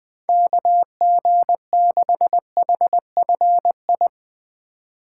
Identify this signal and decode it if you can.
Morse code, heard as a keyed tone, about 20 wpm, 700 Hz. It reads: KG6HFI
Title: USA Call Signs - 4x Character Spacing - 20wpm